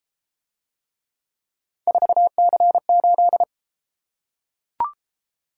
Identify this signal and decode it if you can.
Morse code, heard as a keyed tone, about 31 wpm, 700 Hz. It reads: 4C8